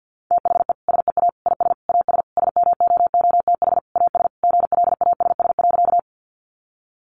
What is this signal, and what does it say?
Morse code, heard as a keyed tone, about 50 wpm, 700 Hz. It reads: THE SEA IS AS SMOOTH AS GLASS.
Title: Sentences from Top 1400 Words 50wpm